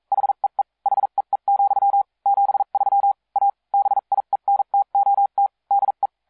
Morse code, heard as a keyed tone, about 42 wpm, spaced as for 36 wpm, 800 Hz. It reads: HEE HEE, 73 A BIENTOT DE